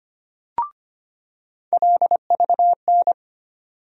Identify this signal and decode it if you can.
Morse code, heard as a keyed tone, about 25 wpm, 700 Hz. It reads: LVN
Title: USA Call Sign Postfixes - 3 Letters 25wpm